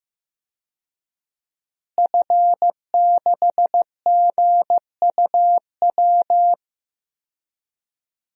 Morse code, heard as a keyed tone, about 15 wpm, 700 Hz. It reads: F6GUW
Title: DXCC-100 Call Signs - 5 Characters - Repeated 2x - 15wpm